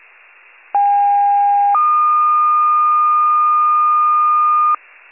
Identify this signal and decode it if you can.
Two-tone page, heard as a one-second tone, then a three-second tone, about 810 then 1220 Hz. It is heard from 0.7 to 4.8 s.